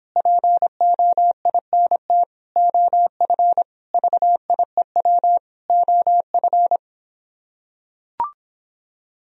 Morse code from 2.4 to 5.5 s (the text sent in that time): OF VIEW